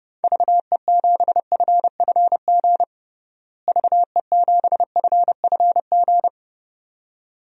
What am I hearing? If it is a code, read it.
Morse code, heard as a keyed tone, about 30 wpm, 700 Hz. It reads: VE7FFG VE7FFG
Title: DXCC-100 Call Signs - 6 Characters - Repeated 2x - 30wpm